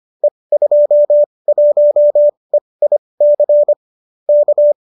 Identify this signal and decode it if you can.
Morse code, heard as a keyed tone, about 25 wpm, spaced as for 20 wpm, 590 Hz. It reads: E21EIC K
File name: qso208 20